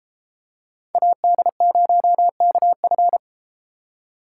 Morse code, heard as a keyed tone, about 33 wpm, 700 Hz. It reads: AD0KF